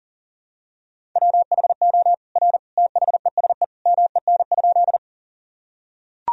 Morse code, heard as a keyed tone, about 40 wpm, 700 Hz. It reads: WHO R THESE MEN?
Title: Sentences from Top 200 Words and Abbreviations 40wpm